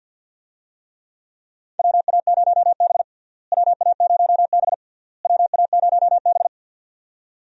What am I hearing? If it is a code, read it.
Morse code, heard as a keyed tone, about 52 wpm, 700 Hz. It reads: WA0B WA0B WA0B
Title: USA Call Signs - 4 Characters - Repeated 3x - 50wpm